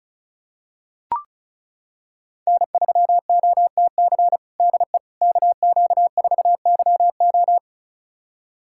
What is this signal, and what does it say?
Morse code, heard as a keyed tone, about 36 wpm, 700 Hz. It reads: N3OTC DE KQ4YO
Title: USA Callsign DE Callsign - 5 Characters - 35wpm